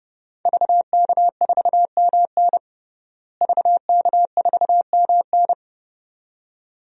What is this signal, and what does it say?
Morse code, heard as a keyed tone, about 30 wpm, 700 Hz. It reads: VK4MN VK4MN
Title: DXCC-100 Call Signs - 5 Characters - Repeated 2x - 30wpm